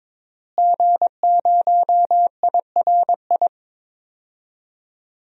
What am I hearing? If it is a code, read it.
Morse code, heard as a keyed tone, about 21 wpm, 700 Hz. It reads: G0IRI
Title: DXCC-100 Call Signs - 5 Characters - Repeated 3x - 22wpm